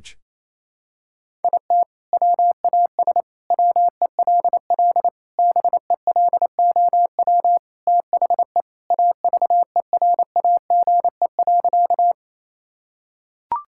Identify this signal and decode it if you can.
Morse code, heard as a keyed tone, about 27 wpm, 700 Hz. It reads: IT WAS WELL BELOW THE AVERAGE.
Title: Sentences from Top 1000 Words - Review Part 6 - 28wpm